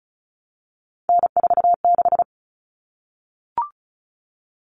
Morse code, heard as a keyed tone, about 34 wpm, 700 Hz. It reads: N46
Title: Common Cut Numbers - 3-Digit 35wpm